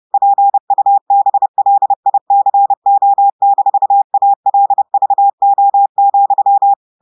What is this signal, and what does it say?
Morse code, heard as a keyed tone, about 30 wpm, 800 Hz. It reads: PUBLICO-ALVO,
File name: audios / portugais / portugais texte en clair 2 vitesse 30 mots minutes